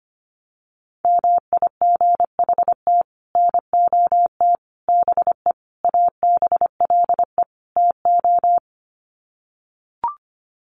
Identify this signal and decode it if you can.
Morse code, heard as a keyed tone, about 25 wpm, 700 Hz. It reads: MIGHT NOT BE ABLE TO
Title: Sets of 5 Words Encore2 25wpm